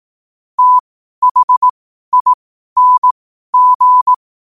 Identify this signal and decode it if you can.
Morse code, heard as a keyed tone, about 18 wpm, spaced as for 12 wpm, 1000 Hz. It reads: THING
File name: qotdfiles / 12wpm